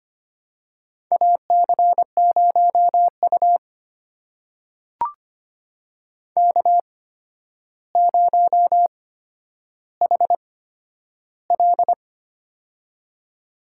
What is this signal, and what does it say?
Morse code, heard as a keyed tone, about 25 wpm, spaced as for 11 wpm, 700 Hz. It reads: AC0U K0HL
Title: USA Call Signs - 4 Characters - 8x Character Spacing - 25wpm